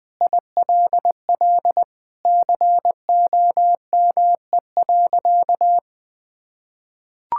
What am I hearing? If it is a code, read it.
Morse code, heard as a keyed tone, about 20 wpm, 700 Hz. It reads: ILL COME.
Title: Sentences with 2-Word Phrases - Spoken Beginning - 20wpm